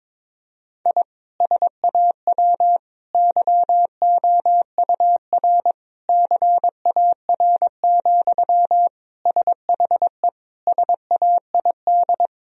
Morse code, heard as a keyed tone, about 22 wpm, 700 Hz. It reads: I SAW YOUR CAR, SHE SAID
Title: Sentences from Top 1000 Words - Review Part 1 - 22wpm